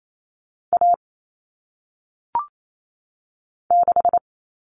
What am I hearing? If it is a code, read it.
Morse code, heard as a keyed tone, about 28 wpm, 700 Hz. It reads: A 6
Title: Common Cut Numbers - 1-Digit 28wpm